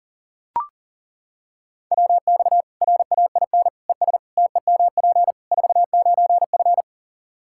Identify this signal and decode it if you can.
Morse code, heard as a keyed tone, about 40 wpm, 700 Hz. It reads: WX RAIN ES TEMP 49F
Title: Practice QSO - Exchange 2 - Info - 40wpm